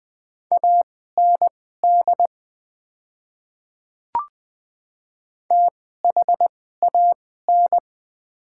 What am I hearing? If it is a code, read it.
Morse code, heard as a keyed tone, about 20 wpm, spaced as for 15 wpm, 690 Hz. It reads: AND THAN
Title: 2x Character Spacing - Top 100 Words - 20wpm